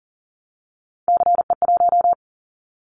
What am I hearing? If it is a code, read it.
Morse code, heard as a keyed tone, about 38 wpm, 700 Hz. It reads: CE1
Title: Callsign Prefixes for Top 100 DXCC - 40wpm